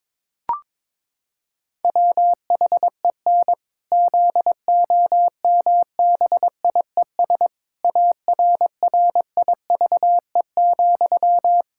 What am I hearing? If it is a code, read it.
Morse code, heard as a keyed tone, about 22 wpm, 700 Hz. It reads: WHEN ZOMBIES ARRIVE,
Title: Pangram Challenge v2 - 22wpm